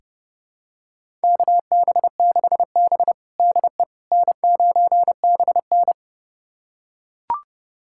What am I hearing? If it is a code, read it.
Morse code, heard as a keyed tone, about 30 wpm, 700 Hz. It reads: KB6B DE N9BN
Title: USA Callsign DE Callsign - 4 Characters - 30wpm